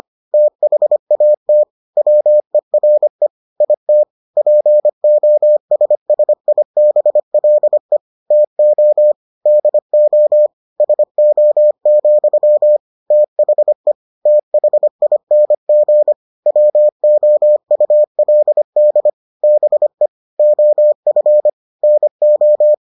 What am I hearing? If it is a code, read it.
Morse code, heard as a keyed tone, about 25 wpm, 600 Hz. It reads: THAT WERE IT POSSIBLE TO DO SO, THE THING WOULD BE OF NO